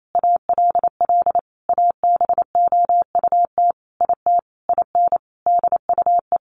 Morse code, heard as a keyed tone, about 28 wpm, 700 Hz. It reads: ALL ABOUT IT IN DUE
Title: Sentences from Top 900 Words 28wpm